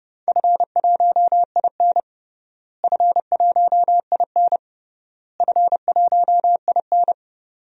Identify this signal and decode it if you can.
Morse code, heard as a keyed tone, about 30 wpm, 700 Hz. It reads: F1IN F1IN F1IN
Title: DXCC-100 Call Signs - 4 Characters - Repeated 3x - 30wpm